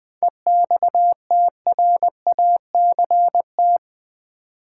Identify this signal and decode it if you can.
Morse code, heard as a keyed tone, about 20 wpm, 700 Hz. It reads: EXTRACT
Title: 7 Letter Words - First 3 Letters Given - 20wpm